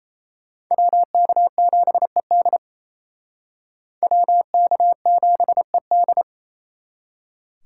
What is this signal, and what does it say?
Morse code, heard as a keyed tone, about 36 wpm, 700 Hz. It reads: WK7ED WK7ED